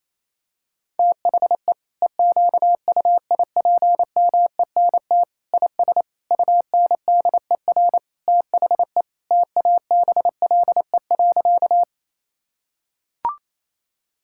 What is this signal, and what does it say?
Morse code, heard as a keyed tone, about 28 wpm, 700 Hz. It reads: THE EQUIPMENT IS UNDER THE TABLE.
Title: Sentences from Top 1000 Words - Review Part 5 - 28wpm